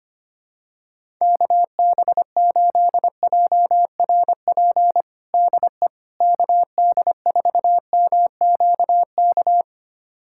Morse code, heard as a keyed tone, about 25 wpm, 700 Hz. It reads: KB8JRP DE KD4MQK
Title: USA Callsign DE Callsign - 6 Characters - 25wpm